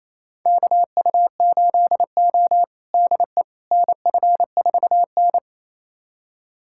Morse code, heard as a keyed tone, about 28 wpm, 700 Hz. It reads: KU8O DE NF4N
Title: USA Callsign DE Callsign - 4 Characters - 28wpm